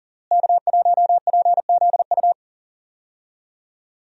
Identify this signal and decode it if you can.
Morse code, heard as a keyed tone, about 40 wpm, 700 Hz. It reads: K1PZU
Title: USA Call Signs - 5 Characters - Repeated 2x - 40wpm